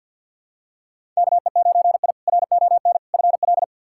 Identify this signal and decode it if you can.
Morse code, heard as a keyed tone, about 52 wpm, 700 Hz. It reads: KE9I RON FL
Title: K1USN SST QSO Practice - 50wpm